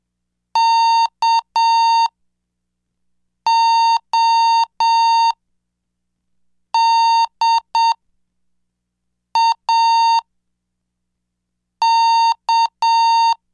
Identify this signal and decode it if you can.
Morse code, heard as a keyed tone, about 7 wpm, spaced as for 4 wpm, 920 Hz. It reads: KODAK
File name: cursos / CDestudo / 11 Track 11